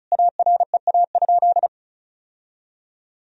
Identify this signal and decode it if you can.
Morse code, heard as a keyed tone, about 36 wpm, 700 Hz. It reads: AREA?
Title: Sentences from Top 1000 Words - Review Part 2 - 35wpm